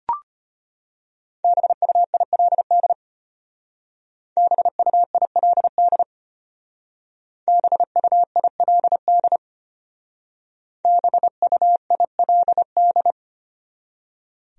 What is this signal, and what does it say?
Morse code, heard as a keyed tone, about 32 wpm, 700 Hz. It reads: BUILD BUILD BUILD BUILD